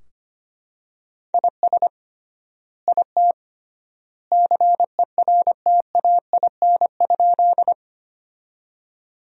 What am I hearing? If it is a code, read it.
Morse code, heard as a keyed tone, about 25 wpm, 700 Hz. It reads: IS IT CERTAIN?
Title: 3x Word-Spacing - Sentences from Top 500 Words - 25wpm